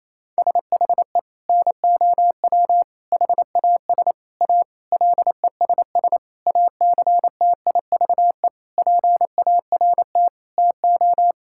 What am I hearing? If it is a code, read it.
Morse code, heard as a keyed tone, about 27 wpm, 700 Hz. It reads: SHE NOW HAS A LESS ACTIVE PART TO
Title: Sentences from Top 1000 Words - Review Part 6 - 28wpm